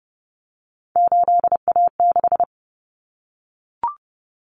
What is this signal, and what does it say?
Morse code, heard as a keyed tone, about 30 wpm, 700 Hz. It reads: 8A6
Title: Common Cut Numbers - 3-Digit 30wpm